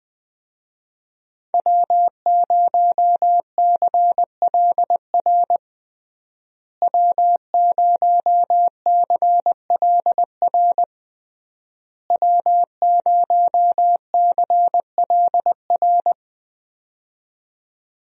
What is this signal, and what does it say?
Morse code, heard as a keyed tone, about 20 wpm, 700 Hz. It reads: W0CLR W0CLR W0CLR
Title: USA Call Signs - 5 Characters - Repeated 3x - 20wpm